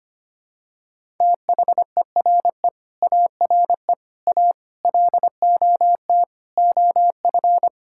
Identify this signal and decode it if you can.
Morse code, heard as a keyed tone, about 25 wpm, 700 Hz. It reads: THERE ARE A LOT OF
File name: Sets of 9 Words - 25wpm